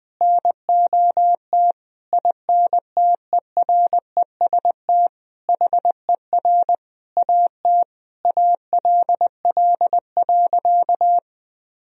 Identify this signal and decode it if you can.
Morse code, heard as a keyed tone, about 20 wpm, 700 Hz. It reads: NOT INTEREST HER AT ALL.
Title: Sentences from Top 1000 Words - Review Part 2 - 20wpm